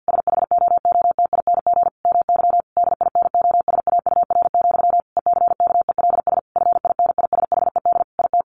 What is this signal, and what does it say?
Morse code, heard as a keyed tone, about 50 wpm, 700 Hz. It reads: SHOOTING MY DINOSAUR, ECKELS FINISHED IT
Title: A Sound Of Thunder 50wpm